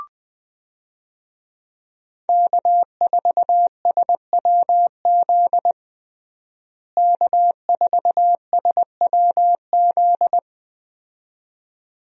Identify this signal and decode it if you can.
Morse code, heard as a keyed tone, about 20 wpm, 700 Hz. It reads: K4SWZ K4SWZ
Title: USA Call Signs - 5 Characters - Repeated 2x - 20wpm